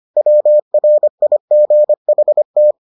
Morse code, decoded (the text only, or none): WRIGHT